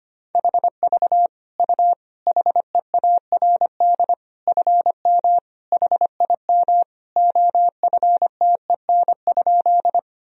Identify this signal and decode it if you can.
Morse code, heard as a keyed tone, about 25 wpm, 700 Hz. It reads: HV U HEARD FM HIM OFTEN?
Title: Sentences from Top 400 Words and Abbreviations 25wpm